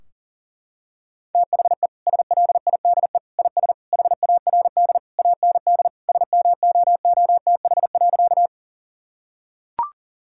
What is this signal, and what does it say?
Morse code, heard as a keyed tone, about 40 wpm, 700 Hz. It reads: THE SLIDE IS HARD AND SMOOTH.
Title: Sentences from Top 1600 Words 40wpm